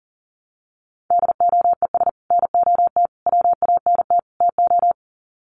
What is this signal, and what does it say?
Morse code, heard as a keyed tone, about 40 wpm, 700 Hz. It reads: DOES NOT WANT TO
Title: Sets of 4 Words 40wpm